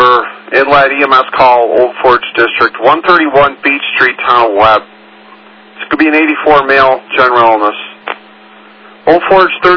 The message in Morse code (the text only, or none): none